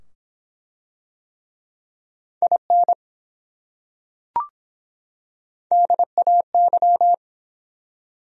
Morse code, heard as a keyed tone, about 23 wpm, 700 Hz. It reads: IN DAY